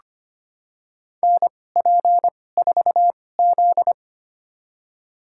Morse code, heard as a keyed tone, about 25 wpm, spaced as for 18 wpm, 700 Hz. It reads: NP4Z
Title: USA Call Signs - 2x Character Spacing - 25wpm